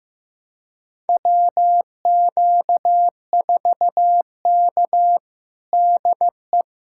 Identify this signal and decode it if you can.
Morse code, heard as a keyed tone, about 16 wpm, 700 Hz. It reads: WQ4K DE
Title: USA Callsign DE Callsign - 4 Characters - 15wpm